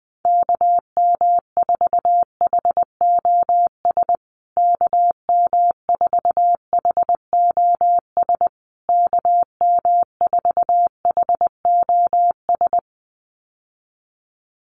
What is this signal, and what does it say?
Morse code, heard as a keyed tone, about 20 wpm, 700 Hz. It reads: KM4HOS KM4HOS KM4HOS
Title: USA Call Signs Repeated 3x 20wpm